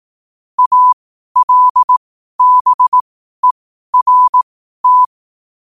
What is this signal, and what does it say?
Morse code, heard as a keyed tone, about 18 wpm, spaced as for 12 wpm, 1000 Hz. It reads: ALBERT